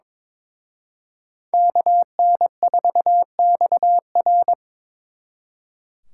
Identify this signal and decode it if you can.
Morse code, heard as a keyed tone, about 26 wpm, 700 Hz. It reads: KN4XR
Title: Speed Racing - USA 2x2 Call Signs 22wpm